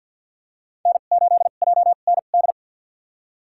Morse code, heard as a keyed tone, about 53 wpm, spaced as for 37 wpm, 700 Hz. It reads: N8JND
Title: USA Call Signs - 5 Characters - 2x Character Spacing - 50wpm